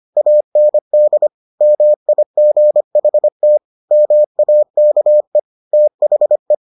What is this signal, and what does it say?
Morse code, heard as a keyed tone, about 25 wpm, 600 Hz. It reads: AND MIGHT MAKE THE